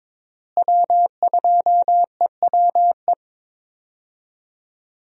Morse code, heard as a keyed tone, about 22 wpm, 700 Hz. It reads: W2EWE
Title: USA Call Signs - 5 Characters - Repeated 3x - 22wpm